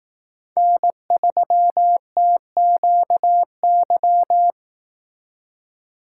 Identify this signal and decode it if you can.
Morse code, heard as a keyed tone, about 18 wpm, 700 Hz. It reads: N3TQY
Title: Speed Racing - USA 1x3 Call Signs 15wpm